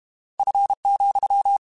Morse code, heard as a keyed tone, about 32 wpm, 800 Hz. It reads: F,
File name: Weather Sets / Weather Sets-mp3s / Weather 5 1 @30